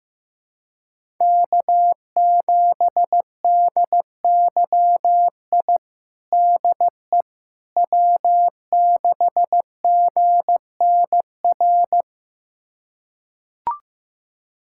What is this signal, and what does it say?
Morse code, heard as a keyed tone, about 16 wpm, 700 Hz. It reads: K7DYI DE W6GNR
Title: USA Callsign DE Callsign - 5 Characters - 15wpm